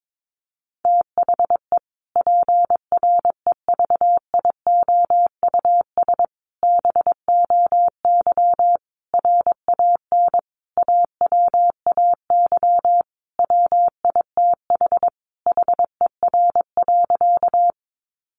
Morse code, heard as a keyed tone, about 22 wpm, 700 Hz. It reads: THE PREVIOUS BOY RAN AWAY WITH HER.
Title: Sentences from Top 1500 Words 22wpm